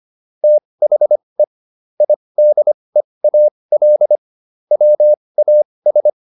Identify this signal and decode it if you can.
Morse code, heard as a keyed tone, about 25 wpm, spaced as for 20 wpm, 600 Hz. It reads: THE IDEAL WAS